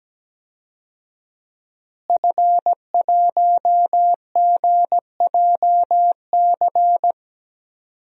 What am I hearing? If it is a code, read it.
Morse code, heard as a keyed tone, about 17 wpm, 700 Hz. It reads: F1GJC
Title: DXCC-100 Call Signs - 5 Characters - Repeated 3x - 17wpm